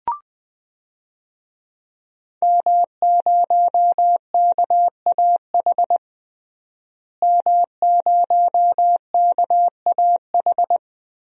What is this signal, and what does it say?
Morse code, heard as a keyed tone, about 20 wpm, 700 Hz. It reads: M0KAH M0KAH
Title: DXCC-100 Call Signs - 5 Characters - Repeated 3x - 20wpm